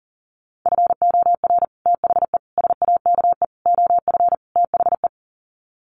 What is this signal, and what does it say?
Morse code, heard as a keyed tone, about 37 wpm, 700 Hz. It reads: FOR THE SAKE OF THE
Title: Sets of 5 Words Encore 40wpm